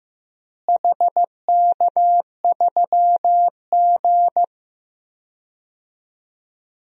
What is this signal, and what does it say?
Morse code, heard as a keyed tone, about 15 wpm, 700 Hz. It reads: HK3G
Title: DXCC-100 Call Signs - 4 Characters - Repeated 3x - 15wpm